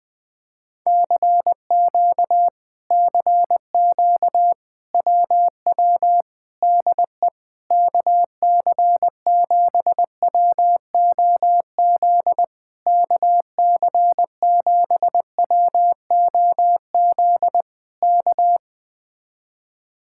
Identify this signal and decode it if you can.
Morse code, heard as a keyed tone, about 20 wpm, 700 Hz. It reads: CQ CQ WW DE KC7WOZ KC7WOZ K